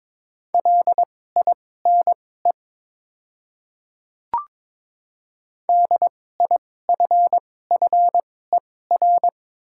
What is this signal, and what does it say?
Morse code, heard as a keyed tone, about 22 wpm, spaced as for 16 wpm, 700 Hz. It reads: LINE DIFFER
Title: 2x Character Spacing - Top 200 Words - 22wpm